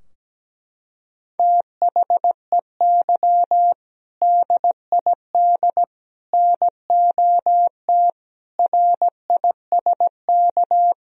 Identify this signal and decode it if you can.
Morse code, heard as a keyed tone, about 17 wpm, 700 Hz. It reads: THEY DID NOT RISK
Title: Sentences from Top 1000 Words - Review Part 4 - 17wpm